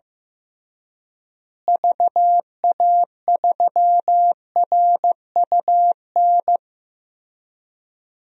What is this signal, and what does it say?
Morse code, heard as a keyed tone, about 15 wpm, 700 Hz. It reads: VA3RUN
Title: DXCC-100 Call Signs - 6 Characters - Repeated 3x - 15wpm